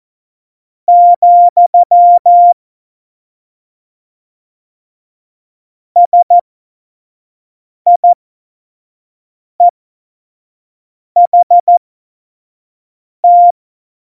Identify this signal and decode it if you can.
Morse code, heard as a keyed tone, about 14 wpm, spaced as for 4 wpm, 700 Hz. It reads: ,SIEHT
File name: Morsekurs / Witze / Witz 22. Oktober 2019, 25 F70